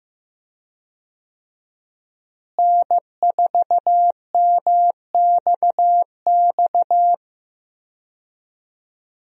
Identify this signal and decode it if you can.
Morse code, heard as a keyed tone, about 15 wpm, 700 Hz. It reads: N4MXX
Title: USA Call Signs - 5 Characters - Repeated 3x - 15wpm